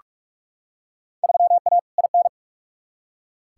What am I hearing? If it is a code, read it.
Morse code, heard as a keyed tone, about 46 wpm, 700 Hz. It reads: 3A IN
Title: Field Day Exchanges - 45wpm